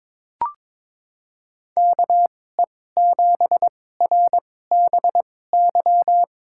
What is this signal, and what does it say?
Morse code, heard as a keyed tone, about 22 wpm, spaced as for 16 wpm, 700 Hz. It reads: KE7RBY
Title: USA Call Signs - 6 Characters - 2x Character Spacing - 22wpm